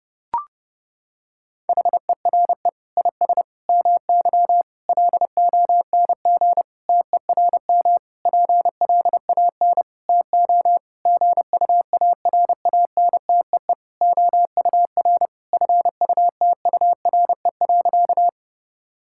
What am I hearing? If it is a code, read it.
Morse code, heard as a keyed tone, about 32 wpm, 700 Hz. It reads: HERE IS MY LONG TERM PLAN TO GUARANTEE OUR FUTURE.
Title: Sentences from Top 1600 Words 30wpm